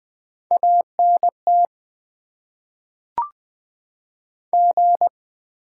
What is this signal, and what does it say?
Morse code, heard as a keyed tone, about 20 wpm, 700 Hz. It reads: ANT G
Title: Top 20 QSO Elements - 8x Character Spacing - 20wpm